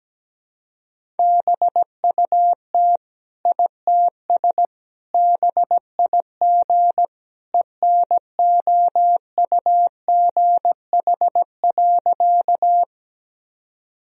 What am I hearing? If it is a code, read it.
Morse code, heard as a keyed tone, about 17 wpm, 700 Hz. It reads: BUT ITS BIG ENOUGH.
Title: Sentences from Top 1000 Words - Review Part 1 - 17wpm